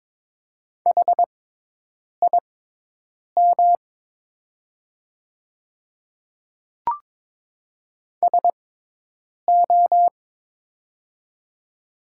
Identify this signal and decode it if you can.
Morse code, heard as a keyed tone, about 22 wpm, spaced as for 8 wpm, 700 Hz. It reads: HIM SO
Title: 6x Character Spacing - Top 200 Words - 22wpm